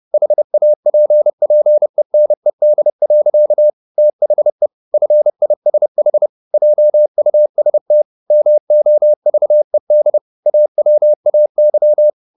Morse code, heard as a keyed tone, about 30 wpm, 600 Hz. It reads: HAPPENED. THE FISH JUST MOVED AWAY
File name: the old man and the sea / the old man and the sea 1 30wpm